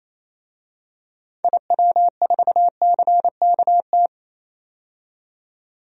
Morse code, heard as a keyed tone, about 28 wpm, 700 Hz. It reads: IW4CKT
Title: DXCC-100 Call Signs - 6 Characters - 28wpm